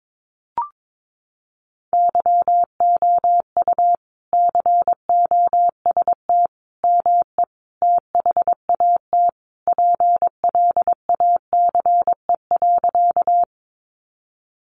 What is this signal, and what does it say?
Morse code, heard as a keyed tone, about 22 wpm, 700 Hz. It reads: YOU COST ME THAT PLACE.
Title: Sentences from Top 800 Words 22wpm